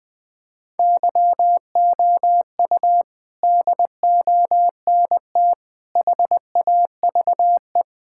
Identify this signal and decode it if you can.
Morse code, heard as a keyed tone, about 21 wpm, 700 Hz. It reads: YOU DONT HAVE
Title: Sentences with 8-Word Phrases - Spoken Beginning - 20wpm